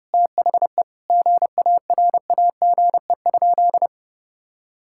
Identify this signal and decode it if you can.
Morse code, heard as a keyed tone, about 30 wpm, 700 Hz. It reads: THE GARAGE?
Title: Sentences from Top 1600 Words 30wpm